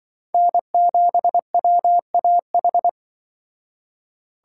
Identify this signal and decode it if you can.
Morse code, heard as a keyed tone, about 24 wpm, 700 Hz. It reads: N7WAH
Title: Speed Racing - USA 1x3 Call Signs 20wpm